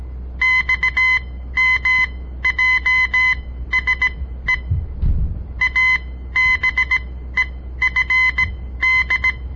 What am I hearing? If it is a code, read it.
Morse code, heard as a keyed tone, about 17 wpm, spaced as for 13 wpm, 2050 Hz. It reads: XMJSE ABEFD
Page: On the air Morse code practice from the Southern Montana Amatuer Radio Association; all code, no read back